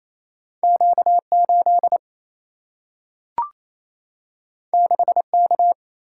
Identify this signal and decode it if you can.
Morse code, heard as a keyed tone, about 28 wpm, 700 Hz. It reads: Q 8 6 K
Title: Single-Letter Single-Number 28wpm